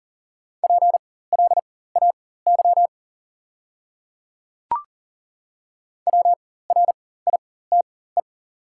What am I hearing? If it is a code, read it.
Morse code, heard as a keyed tone, about 40 wpm, spaced as for 19 wpm, 700 Hz. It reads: PLAY WRITE
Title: 4x Character Spacing - Top 300 Words - 40wpm